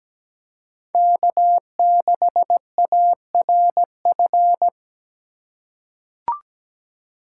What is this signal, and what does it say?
Morse code, heard as a keyed tone, about 17 wpm, 700 Hz. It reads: K6ARF
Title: USA Call Signs - 5 Characters - 2x Character Spacing - 17wpm